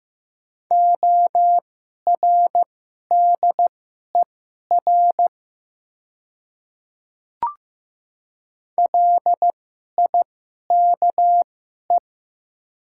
Morse code, heard as a keyed tone, about 15 wpm, spaced as for 11 wpm, 700 Hz. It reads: ORDER LIKE
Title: 2x Character Spacing - Top 500 Words - 15wpm